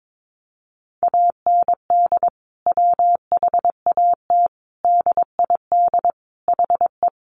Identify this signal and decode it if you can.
Morse code, heard as a keyed tone, about 22 wpm, 700 Hz. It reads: AND WHAT DID HE
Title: Sentences from Top 700 Words 22wpm